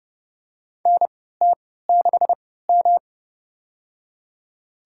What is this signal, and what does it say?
Morse code, heard as a keyed tone, about 30 wpm, spaced as for 17 wpm, 700 Hz. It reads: NT6M
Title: USA Call Signs - 3x Character Spacing - 30wpm